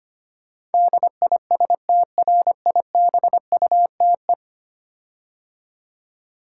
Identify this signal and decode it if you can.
Morse code, heard as a keyed tone, about 25 wpm, 700 Hz. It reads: DISTRIBUTE
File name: 10 Letter Words - First 3 Letters Given - 25wpm